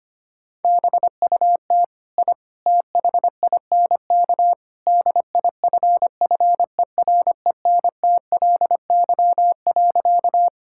Morse code, heard as a keyed tone, about 25 wpm, 700 Hz. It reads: BUT I THINK DIFFERENTLY.
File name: practice / 25wpm / Sentences with 3-Word Phrases - 25wpm